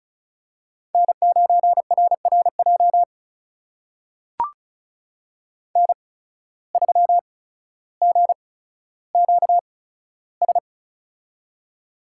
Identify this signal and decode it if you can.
Morse code, heard as a keyed tone, about 36 wpm, spaced as for 15 wpm, 700 Hz. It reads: N9RRJ N3GQS
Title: USA Call Signs - 5 Characters - 8x Character Spacing - 35wpm